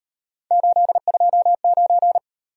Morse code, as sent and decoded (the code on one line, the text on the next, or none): ---.. ..--- ----.
829